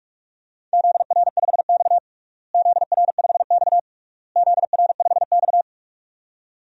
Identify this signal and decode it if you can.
Morse code, heard as a keyed tone, about 46 wpm, 700 Hz. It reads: ZR5X ZR5X ZR5X
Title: DXCC-100 Call Signs - 4 Characters - Repeated 3x - 45wpm